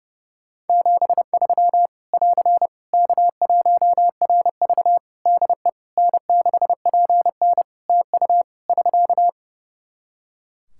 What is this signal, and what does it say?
Morse code, heard as a keyed tone, about 28 wpm, 700 Hz. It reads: 73 <AR> K1RV DE N6PN TU <SK>
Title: Practice QSO - Exchange 4 - Ending - 30wpm